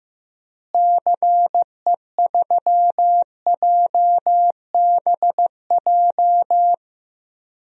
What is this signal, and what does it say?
Morse code, heard as a keyed tone, about 16 wpm, 700 Hz. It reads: CE3JBJ
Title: DXCC-100 Call Signs - 6 Characters - Repeated 2x - 15wpm